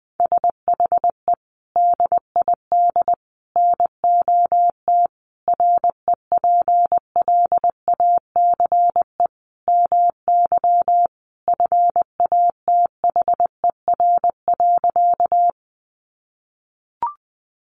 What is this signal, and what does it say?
Morse code, heard as a keyed tone, about 19 wpm, 700 Hz. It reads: SHE DID NOT REPLACE MY FATHER.
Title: Sentences from Top 1400 Words 20wpm